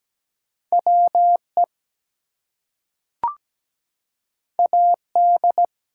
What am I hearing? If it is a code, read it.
Morse code, heard as a keyed tone, about 17 wpm, 700 Hz. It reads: WE AD
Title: Two Letter Words - 17wpm